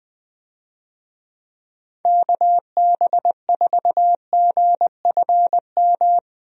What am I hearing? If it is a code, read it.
Morse code, heard as a keyed tone, about 21 wpm, 700 Hz. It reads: KB4GFM
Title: USA Call Signs - 6 Characters - Repeated 2x - 20wpm